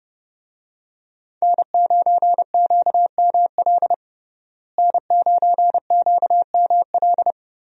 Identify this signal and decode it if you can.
Morse code, heard as a keyed tone, about 32 wpm, 700 Hz. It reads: N9QML N9QML
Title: USA Call Signs - 5 Characters - Repeated 2x - 30wpm